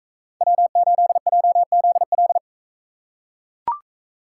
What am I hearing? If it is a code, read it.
Morse code, heard as a keyed tone, about 44 wpm, 700 Hz. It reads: W8JZL